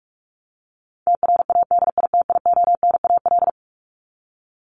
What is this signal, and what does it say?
Morse code, heard as a keyed tone, about 44 wpm, 700 Hz. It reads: TRADITIONAL
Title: Top 751-1000 Words 45wpm